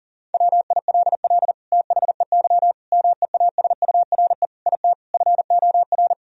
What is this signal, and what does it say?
Morse code, heard as a keyed tone, about 37 wpm, 700 Hz. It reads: WILL THEY MEASURE IT FOR
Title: Sentences from Top 1000 Words - Review Part 2 - 40wpm